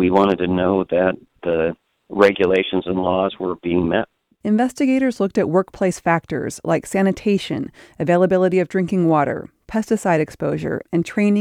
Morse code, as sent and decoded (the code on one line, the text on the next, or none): none